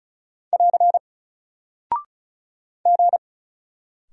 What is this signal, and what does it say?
Morse code, heard as a keyed tone, about 36 wpm, 700 Hz. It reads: <AR> G
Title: Single Character (Letter:Number:Punctuation:Prosign) Alt 35wpm